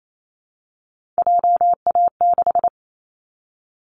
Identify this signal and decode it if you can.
Morse code, heard as a keyed tone, about 28 wpm, 700 Hz. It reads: JA6
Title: Callsign Prefixes for Top 25 DXCC - 28wpm